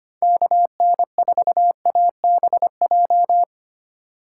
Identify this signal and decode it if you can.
Morse code, heard as a keyed tone, about 25 wpm, 700 Hz. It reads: KN4ABJ
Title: USA Call Signs - 6 Characters - 2x Character Spacing - 25wpm